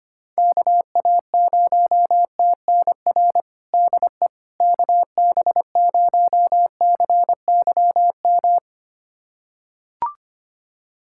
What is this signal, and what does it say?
Morse code, heard as a keyed tone, about 25 wpm, 700 Hz. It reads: KA0TNR DE KB0CYM
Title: USA Callsign DE Callsign 25wpm